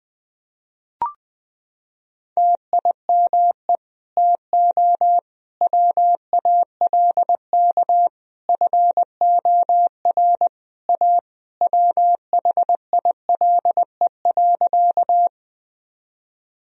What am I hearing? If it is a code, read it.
Morse code, heard as a keyed tone, about 20 wpm, 700 Hz. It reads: TIME TO WALK FOR A WHILE.
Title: Sentences from Top 1000 Words - Review Part 1 - 20wpm